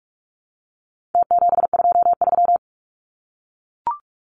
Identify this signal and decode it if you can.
Morse code, heard as a keyed tone, about 44 wpm, 700 Hz. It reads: T723
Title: Numbers - Contest Style - 4-Digit 45wpm